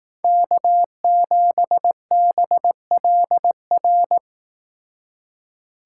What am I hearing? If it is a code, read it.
Morse code, heard as a keyed tone, about 19 wpm, 700 Hz. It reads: K7BLR